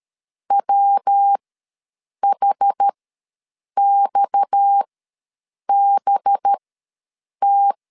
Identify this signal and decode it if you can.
Morse code, heard as a keyed tone, about 13 wpm, spaced as for 7 wpm, 800 Hz. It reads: WHXBT
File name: LEZIONI / 6 PKX0 / 6 PKX ws 12wpm 3106